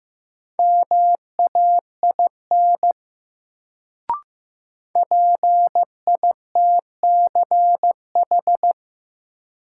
Morse code, heard as a keyed tone, about 15 wpm, 700 Hz. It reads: MAIN PITCH
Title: Sentences from Top 1300 Words - New Word Drill Alt - 15wpm